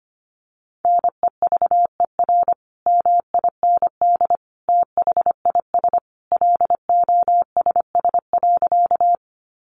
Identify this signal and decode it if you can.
Morse code, heard as a keyed tone, about 25 wpm, 700 Hz. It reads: NEVER MIND THIS LOSS.
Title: Sentences from Top 1400 Words 25wpm